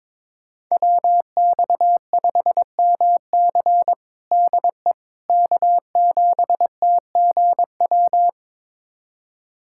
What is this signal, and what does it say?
Morse code, heard as a keyed tone, about 22 wpm, 700 Hz. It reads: WX5MC DE K7TGW
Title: USA Callsign DE Callsign - 5 Characters - 22wpm